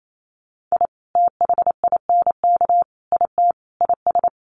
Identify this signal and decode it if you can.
Morse code, heard as a keyed tone, about 28 wpm, 700 Hz. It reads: I THINK IT IS
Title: Sets of 4 Words 28wpm